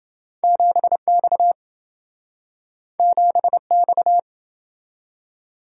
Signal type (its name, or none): Morse code